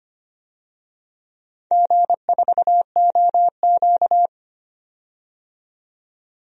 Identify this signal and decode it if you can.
Morse code, heard as a keyed tone, about 25 wpm, 700 Hz. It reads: G4OQ
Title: DXCC-100 Call Signs - 4 Characters - 25wpm